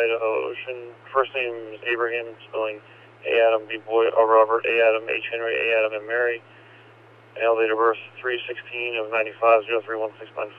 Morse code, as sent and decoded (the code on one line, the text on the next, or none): none